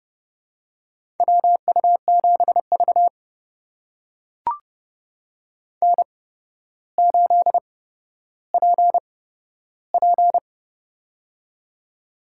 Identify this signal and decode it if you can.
Morse code, heard as a keyed tone, about 31 wpm, spaced as for 13 wpm, 700 Hz. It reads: WU7V N8PP
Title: USA Call Signs - 4 Characters - 8x Character Spacing - 30wpm